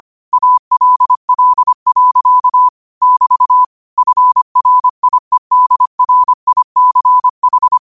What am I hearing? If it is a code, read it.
Morse code, heard as a keyed tone, about 25 wpm, 1000 Hz. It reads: ALL. <BT> FRIEDRICH
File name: qotdfiles / 25wpm